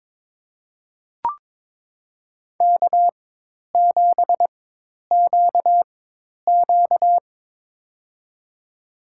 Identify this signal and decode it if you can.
Morse code, heard as a keyed tone, about 22 wpm, spaced as for 10 wpm, 700 Hz. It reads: K7QQ
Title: USA Call Signs - 4 Characters - 4x Character Spacing - 22wpm